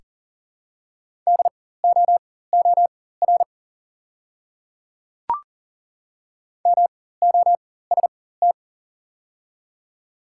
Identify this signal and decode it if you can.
Morse code, heard as a keyed tone, about 40 wpm, spaced as for 19 wpm, 690 Hz. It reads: DOOR MOST
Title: 4x Character Spacing - Top 300 Words - 40wpm